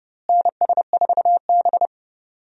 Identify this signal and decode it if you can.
Morse code, heard as a keyed tone, about 32 wpm, 700 Hz. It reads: NS4B
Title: USA Call Signs - 4 Characters - Repeated 3x - 30wpm